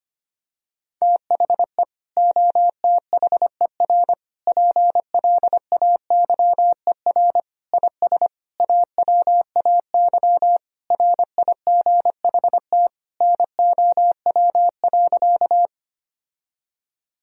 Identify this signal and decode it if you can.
Morse code, heard as a keyed tone, about 25 wpm, 700 Hz. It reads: THE OTHER PLAYER IS AWAY RIGHT NOW.
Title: Sentences from Top 1000 Words - Review Part 4 - 25wpm